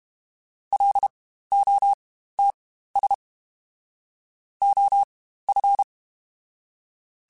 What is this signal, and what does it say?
Morse code, heard as a keyed tone, about 32 wpm, spaced as for 15 wpm, 800 Hz. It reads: LOTS OF
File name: Spaced Characters / Spaced Characters-mp3s / Weather 5 12 sp @30